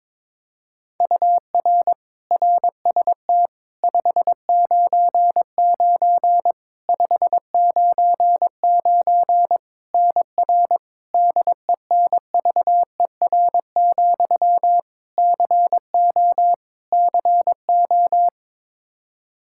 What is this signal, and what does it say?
Morse code, heard as a keyed tone, about 22 wpm, 700 Hz. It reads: UR RST 599 599 NR DENVER, CO CO
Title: Practice QSO - Exchange 1 - Beginning - 22wpm